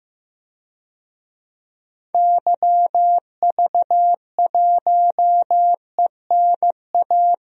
Morse code, heard as a keyed tone, about 15 wpm, 700 Hz. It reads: YV1ENA
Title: DXCC-100 Call Signs - 6 Characters - Repeated 2x - 15wpm